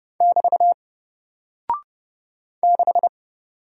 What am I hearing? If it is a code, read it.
Morse code, heard as a keyed tone, about 30 wpm, 700 Hz. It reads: <BT> 6
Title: Single Character (Letter:Number:Punctuation:Prosign) Alt 30wpm